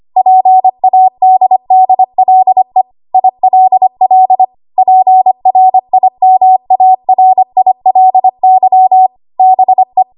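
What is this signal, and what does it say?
Morse code, heard as a keyed tone, about 25 wpm, 750 Hz. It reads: PADDLE ILL PRIMARILY BE
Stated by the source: ARRL 25 WPM code practice archive.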